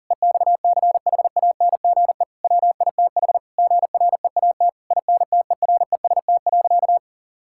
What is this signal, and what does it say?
Morse code, heard as a keyed tone, about 40 wpm, 700 Hz. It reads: EXCHANGE WITH GREAT INTEREST.
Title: Sentences from Top 1000 Words - Review Part 6 - 40wpm